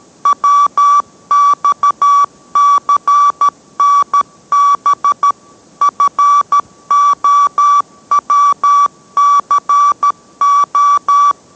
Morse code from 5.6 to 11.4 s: FOWCO